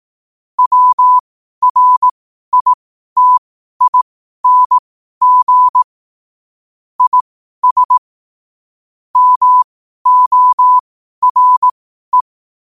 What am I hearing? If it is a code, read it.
Morse code, heard as a keyed tone, about 18 wpm, spaced as for 12 wpm, 1000 Hz. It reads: WRITING IS MORE
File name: qotdfiles / 12wpm